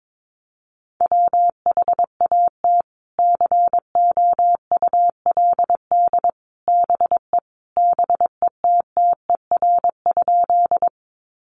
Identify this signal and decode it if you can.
Morse code, heard as a keyed tone, about 22 wpm, 700 Hz. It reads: WHAT COULD BE BETTER?
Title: Sentences from Top 400 Words 22wpm